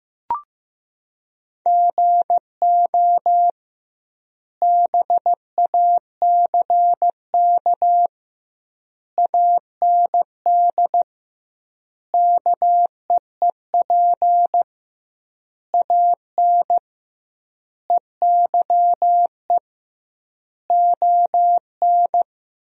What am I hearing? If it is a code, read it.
Morse code, heard as a keyed tone, about 15 wpm, 700 Hz. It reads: GO BACK AND KEEP AN EYE ON
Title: 2x Word-Spacing - Sentences from Top 300 Words - 15wpm